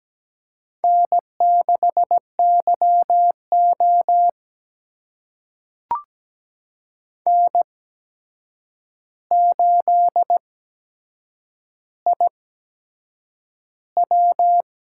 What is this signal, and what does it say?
Morse code, heard as a keyed tone, about 17 wpm, spaced as for 7 wpm, 700 Hz. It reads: N6YO N8IW
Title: USA Call Signs - 8x Character Spacing - 17wpm